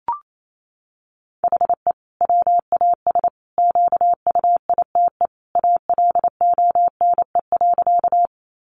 Morse code, heard as a keyed tone, about 28 wpm, 700 Hz. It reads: HE WAS QUITE ALONE.
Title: Sentences from Top 1000 Words 28wpm